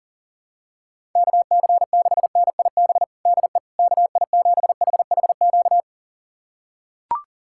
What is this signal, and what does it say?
Morse code, heard as a keyed tone, about 37 wpm, 700 Hz. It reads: KC6NIB DE KI7HHQ
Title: USA Callsign DE Callsign - 6 Characters - 40wpm